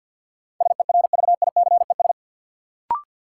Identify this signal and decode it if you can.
Morse code, heard as a keyed tone, about 54 wpm, 700 Hz. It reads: SERVICES
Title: Sets of 5 Words - New Word Drill - 50wpm